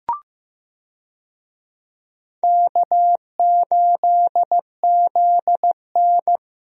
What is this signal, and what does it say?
Morse code, heard as a keyed tone, about 15 wpm, 700 Hz. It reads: K8ZN
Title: USA Call Signs - 4 Characters - Repeated 3x - 15wpm